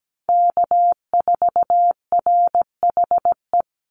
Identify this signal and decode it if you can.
Morse code, heard as a keyed tone, about 17 wpm, 700 Hz. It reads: K4RHE
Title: USA Call Signs 17wpm